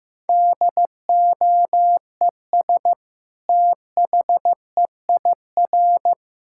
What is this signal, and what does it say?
Morse code, heard as a keyed tone, about 15 wpm, 700 Hz. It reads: DOES THEIR
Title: Sentences from Top 1000 Words - Review Part 5 - 15wpm